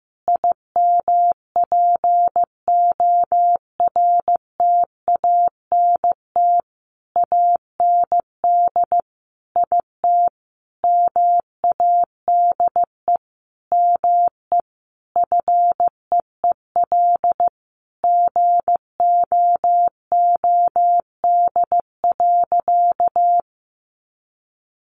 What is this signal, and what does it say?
Morse code, heard as a keyed tone, about 14 wpm, 700 Hz. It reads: IMPORTANT AND IT MADE ME FEEL GOOD.
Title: Sentences from Top 600 Words 15wpm